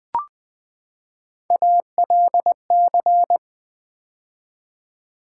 Morse code, heard as a keyed tone, about 20 wpm, 700 Hz. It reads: ALC
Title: Build A Word - 3 Letters At A Time - 9-Letter Words - 20wpm